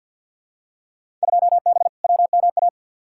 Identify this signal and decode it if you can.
Morse code, heard as a keyed tone, about 51 wpm, 700 Hz. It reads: 2B WMA
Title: Field Day Exchanges Encore - 50wpm